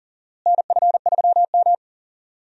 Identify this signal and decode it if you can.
Morse code, heard as a keyed tone, about 40 wpm, 700 Hz. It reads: NF3M
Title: USA Call Signs - 4 Characters - Repeated 2x - 40wpm